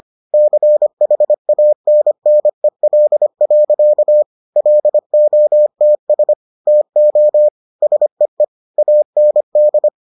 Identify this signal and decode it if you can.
Morse code, heard as a keyed tone, about 25 wpm, 600 Hz. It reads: CHANNEL. LOTS TO SEE AND